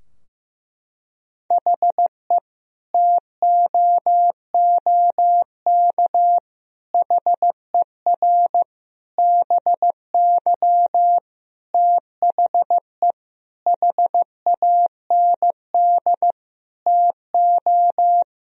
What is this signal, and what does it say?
Morse code, heard as a keyed tone, about 15 wpm, 700 Hz. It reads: HE TOOK HER BY THE HAND TO
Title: Sentences from Top 1000 Words - Review Part 2 - 15wpm